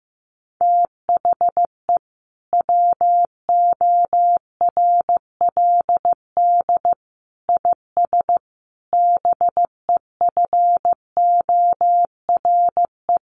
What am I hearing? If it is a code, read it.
Morse code, heard as a keyed tone, about 15 wpm, 700 Hz. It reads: THE WORLD IS BEFORE
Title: Sentences from Top 200 Words 15wpm